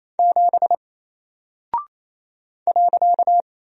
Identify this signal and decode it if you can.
Morse code, heard as a keyed tone, about 28 wpm, 700 Hz. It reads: 7 .
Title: Single Character (Letter:Number:Punctuation:Prosign) Alt 28wpm